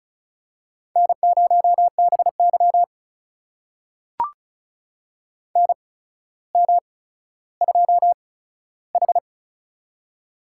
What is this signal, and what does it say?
Morse code, heard as a keyed tone, about 36 wpm, spaced as for 15 wpm, 700 Hz. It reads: N0BY NM2H
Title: USA Call Signs - 4 Characters - 8x Character Spacing - 35wpm